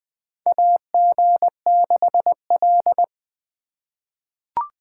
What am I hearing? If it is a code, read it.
Morse code, heard as a keyed tone, about 21 wpm, 700 Hz. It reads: AG6L
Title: USA Call Signs - 4 Characters - 3x Character Spacing - 20wpm